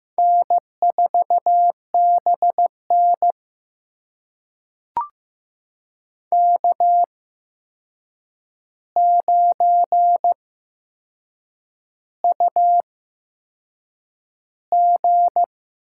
Morse code, heard as a keyed tone, about 15 wpm, spaced as for 6 wpm, 700 Hz. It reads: N4BN K9UG
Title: USA Call Signs - 4 Characters - 8x Character Spacing - 15wpm